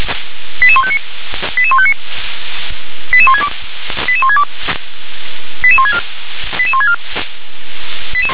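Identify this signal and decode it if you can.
Five-tone selective call (ZVEI), heard as quick runs of five tones, tones of about 70 ms. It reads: 88159 88159 88162 88162 88164 88164
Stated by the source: ZVEI five-tone call received on an SDR.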